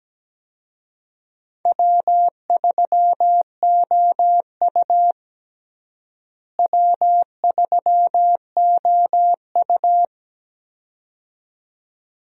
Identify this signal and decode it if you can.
Morse code, heard as a keyed tone, about 17 wpm, 700 Hz. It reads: W3OU W3OU
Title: USA Call Signs - 4 Characters - Repeated 2x - 17wpm